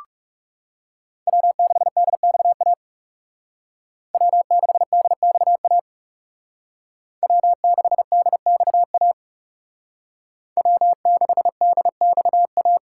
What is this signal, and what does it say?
Morse code, heard as a keyed tone, about 34 wpm, 700 Hz. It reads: W6DXA W6DXA W6DXA W6DXA
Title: Speed Racing - USA Call Signs 30wpm